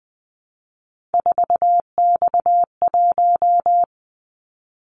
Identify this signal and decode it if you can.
Morse code, heard as a keyed tone, about 20 wpm, 700 Hz. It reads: 4X1
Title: Callsign Prefixes for Top 50 DXCC - 20wpm